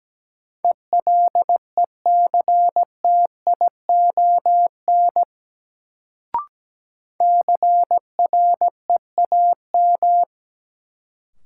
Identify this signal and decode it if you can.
Morse code, heard as a keyed tone, about 17 wpm, 700 Hz. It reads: ELECTION CREAM
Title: Sentences from Top 1500 Words - New Word Drill - 17wpm